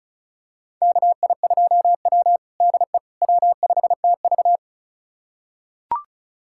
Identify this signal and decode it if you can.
Morse code, heard as a keyed tone, about 36 wpm, 700 Hz. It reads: KI2W DE W5TV
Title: USA Callsign DE Callsign - 4 Characters - 35wpm